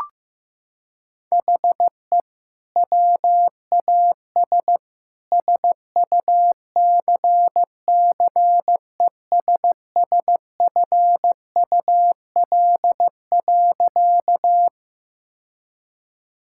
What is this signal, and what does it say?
Morse code, heard as a keyed tone, about 15 wpm, 700 Hz. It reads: HE WAS SUCCESSFUL.
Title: Sentences from Top 1000 Words - Review Part 6 - 15wpm